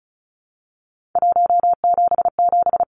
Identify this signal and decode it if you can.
Morse code, heard as a keyed tone, about 34 wpm, 700 Hz. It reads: 177
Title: Numbers - 3-Digit 35wpm